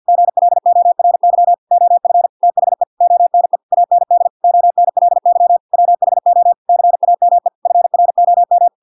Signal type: Morse code